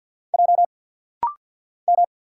Morse code, heard as a keyed tone, about 49 wpm, 700 Hz. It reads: J M
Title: Single Letter-Number Alt - Rapid-Fire 50wpm